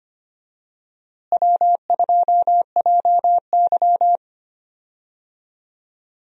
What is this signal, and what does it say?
Morse code, heard as a keyed tone, about 25 wpm, 700 Hz. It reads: W2JY
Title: USA Call Signs - 4 Characters - Repeated 3x - 25wpm